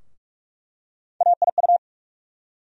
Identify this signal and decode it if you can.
Morse code, heard as a keyed tone, about 45 wpm, 700 Hz. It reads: AIU
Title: USA Call Sign Postfixes - 3 Letters 45wpm